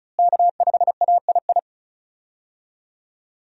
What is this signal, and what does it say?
Morse code, heard as a keyed tone, about 36 wpm, 700 Hz. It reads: K5AII
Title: USA Call Signs - 5 Characters - Repeated 2x - 35wpm